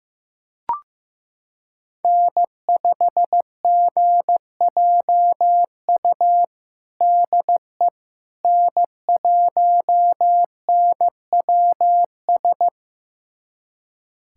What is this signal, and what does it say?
Morse code, heard as a keyed tone, about 15 wpm, 700 Hz. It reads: N5GJU DE N1NWS
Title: USA Callsign DE Callsign - 5 Characters - 15wpm